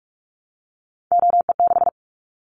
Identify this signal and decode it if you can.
Morse code, heard as a keyed tone, about 46 wpm, 700 Hz. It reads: OE6
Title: Callsign Prefixes for Top 50 DXCC - 45wpm